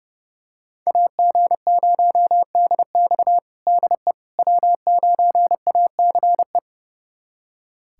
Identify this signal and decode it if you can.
Morse code, heard as a keyed tone, about 32 wpm, 700 Hz. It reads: AG0DX DE W9ACE
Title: USA Callsign DE Callsign - 5 Characters - 30wpm